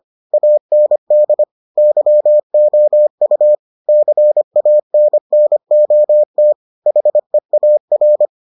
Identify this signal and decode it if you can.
Morse code, heard as a keyed tone, about 25 wpm, 600 Hz. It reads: AND YOU CANNOT HEAR